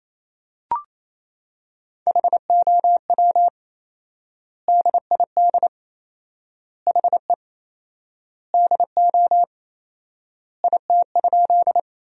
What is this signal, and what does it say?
Morse code, heard as a keyed tone, about 27 wpm, 700 Hz. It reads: HOW DID HE DO IT?
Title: 4x Word-Spacing - Sentences from Top 100 Words - 28wpm